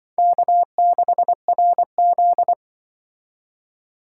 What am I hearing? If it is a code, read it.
Morse code, heard as a keyed tone, about 23 wpm, 700 Hz. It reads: K6RZ
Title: Speed Racing - USA 1x2 Call Signs 20wpm